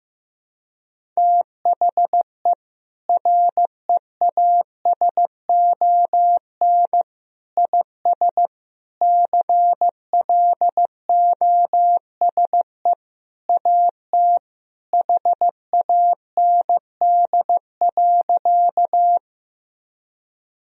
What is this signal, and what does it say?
Morse code, heard as a keyed tone, about 16 wpm, 700 Hz. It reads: THE REASON IS CLOSE AT HAND.
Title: Sentences from Top 1000 Words - Review Part 1 - 15wpm